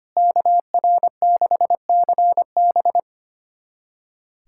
Morse code, heard as a keyed tone, about 25 wpm, 700 Hz. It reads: KR6CB
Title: USA Callsign DE Callsign - 5 Characters - 25wpm